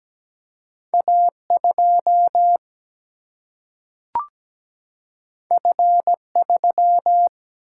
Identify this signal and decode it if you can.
Morse code, heard as a keyed tone, about 17 wpm, 700 Hz. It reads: A2 F3
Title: Single-Letter Single-Number 17wpm